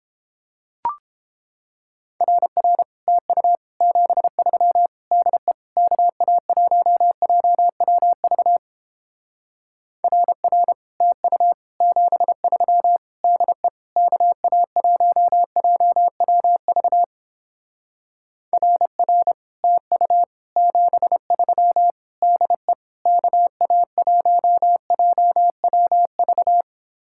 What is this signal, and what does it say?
Morse code, heard as a keyed tone, about 29 wpm, 700 Hz. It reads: RR TU 73 DE KA1JWV RR TU 73 DE KA1JWV RR TU 73 DE KA1JWV